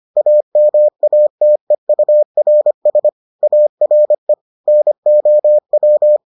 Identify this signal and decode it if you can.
Morse code, heard as a keyed tone, about 25 wpm, 600 Hz. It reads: AMATEURS ARE NOW